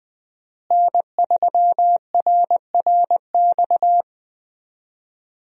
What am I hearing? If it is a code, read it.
Morse code, heard as a keyed tone, about 20 wpm, 700 Hz. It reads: N3RRX
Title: USA Call Signs - 5 Characters - Repeated 2x - 20wpm